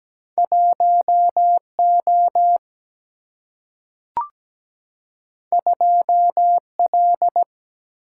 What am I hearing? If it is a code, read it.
Morse code, heard as a keyed tone, about 17 wpm, 700 Hz. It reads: 1O 2L
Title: Single-Letter Single-Number 17wpm